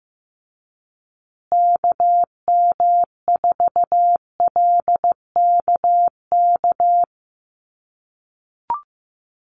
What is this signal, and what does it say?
Morse code, heard as a keyed tone, about 15 wpm, 700 Hz. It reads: KM4LKK
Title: USA Call Signs Repeated 2x 15wpm